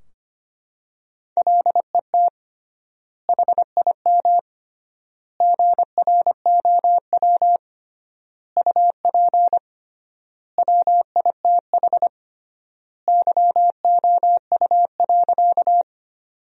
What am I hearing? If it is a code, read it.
Morse code, heard as a keyed tone, about 25 wpm, 700 Hz. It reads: LET HIM GROW UP WITH YOU.
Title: 3x Word-Spacing - Sentences from Top 300 Words - 25wpm